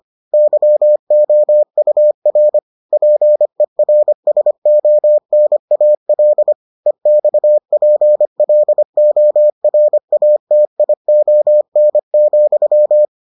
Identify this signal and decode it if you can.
Morse code, heard as a keyed tone, about 25 wpm, 600 Hz. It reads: YOUR PERSONAL EXPLORATION,